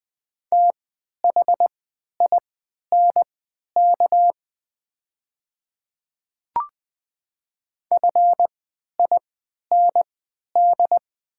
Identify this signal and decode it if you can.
Morse code, heard as a keyed tone, about 20 wpm, spaced as for 11 wpm, 700 Hz. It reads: THINK FIND
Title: 3x Character Spacing - Top 500 Words - 20wpm